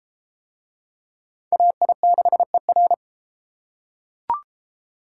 Morse code, heard as a keyed tone, about 31 wpm, 690 Hz. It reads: AI6ER